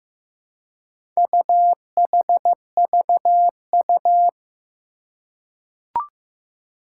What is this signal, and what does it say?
Morse code, heard as a keyed tone, about 15 wpm, 700 Hz. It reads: UHVU
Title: Four Challenging Letter-Number Combinations 15wpm